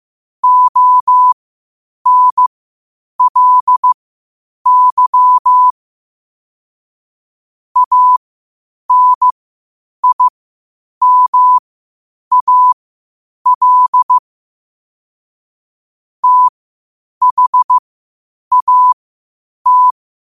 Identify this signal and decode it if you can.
Morse code, heard as a keyed tone, about 15 wpm, spaced as for 8 wpm, 1000 Hz. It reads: ONLY ANIMAL THAT
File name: qotdfiles / 8wpm